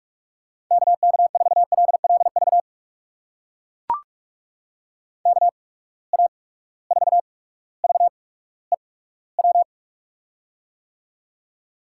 Morse code, heard as a keyed tone, about 44 wpm, spaced as for 19 wpm, 700 Hz. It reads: KK4LLV KA4VEW
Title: USA Call Signs - 6 Characters - 8x Character Spacing - 45wpm